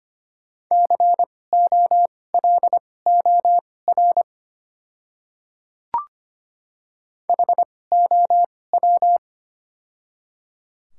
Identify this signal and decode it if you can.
Morse code, heard as a keyed tone, about 25 wpm, spaced as for 18 wpm, 690 Hz. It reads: COLOR HOW
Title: 2x Character Spacing - Top 300 Words - 25wpm